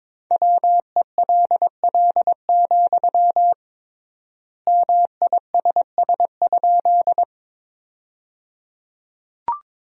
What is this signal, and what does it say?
Morse code, heard as a keyed tone, about 22 wpm, 700 Hz. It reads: WELL, MISS?
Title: 3x Word-Spacing - Sentences from Top 500 Words - 22wpm